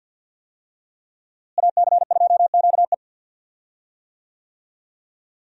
Morse code, heard as a keyed tone, about 46 wpm, 700 Hz. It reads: AC2QE
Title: USA Call Signs - 5 Characters - Repeated 2x - 50wpm